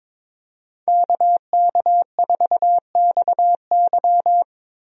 Morse code, heard as a keyed tone, about 22 wpm, 700 Hz. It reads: KK4XY
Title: USA Call Signs - 5 Characters - 22wpm